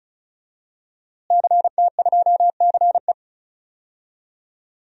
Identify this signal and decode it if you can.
Morse code, heard as a keyed tone, about 34 wpm, 700 Hz. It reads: CT2CE
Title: DXCC-100 Call Signs - 5 Characters - Repeated 3x - 35wpm